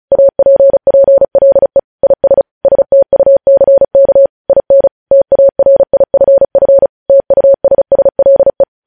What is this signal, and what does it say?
Morse code, heard as a keyed tone, about 36 wpm, 550 Hz. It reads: APPLE IS STUCK IN TARIFF TUSSLE
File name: podcast / News headlines-1x-35x35